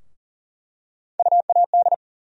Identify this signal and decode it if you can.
Morse code, heard as a keyed tone, about 40 wpm, 700 Hz. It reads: UAD
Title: USA Call Sign Postfixes - 3 Letters 40wpm